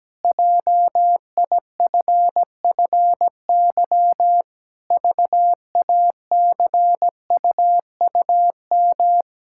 Morse code, heard as a keyed tone, about 17 wpm, 700 Hz. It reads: JIFFY VACUUM
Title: Pangram Challenge v2 - 17wpm